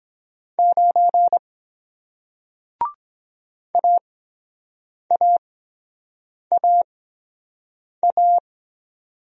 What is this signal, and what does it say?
Morse code, heard as a keyed tone, about 23 wpm, spaced as for 7 wpm, 700 Hz. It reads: 9 AAAA